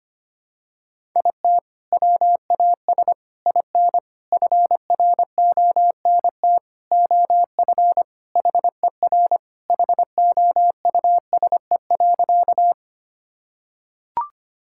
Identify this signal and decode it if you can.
Morse code, heard as a keyed tone, about 25 wpm, 700 Hz. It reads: IT WAS IN FRONT OF HER HOUSE.
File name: practice / 25wpm / Sentences with 4-Word Phrases - 25wpm